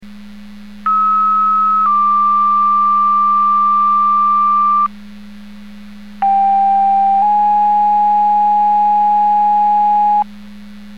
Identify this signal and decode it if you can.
Two-tone page, heard as a one-second tone, then a three-second tone, about 1290 then 1180 Hz, about 790 then 830 Hz. It is heard from 0.9 to 10.2 s.